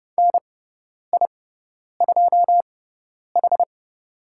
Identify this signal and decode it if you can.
Morse code, heard as a keyed tone, about 30 wpm, spaced as for 10 wpm, 700 Hz. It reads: NI2H
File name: Lesson 16g - Letter c - Call Signs 30wpm